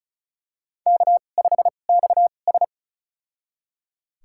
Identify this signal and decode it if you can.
Morse code, heard as a keyed tone, about 36 wpm, spaced as for 25 wpm, 700 Hz. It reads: K5XS
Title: USA Call Signs - 4 Characters - 2x Character Spacing - 35wpm